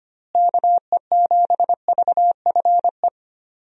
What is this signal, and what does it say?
Morse code, heard as a keyed tone, about 25 wpm, 700 Hz. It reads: KE7VFE
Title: USA Call Signs - 6 Characters - Repeated 3x - 25wpm